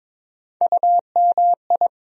Morse code, heard as a keyed tone, about 22 wpm, 700 Hz. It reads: UMI
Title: USA Call Sign Postfixes - 3 Letters 22wpm